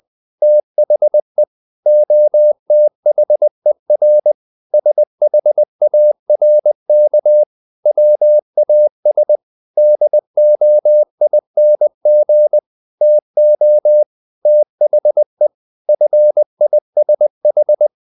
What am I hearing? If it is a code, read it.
Morse code, heard as a keyed tone, about 20 wpm, 600 Hz. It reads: THE OTHER SHARK WAS DOING TO THE FISH
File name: the old man and the sea part 2 20wpm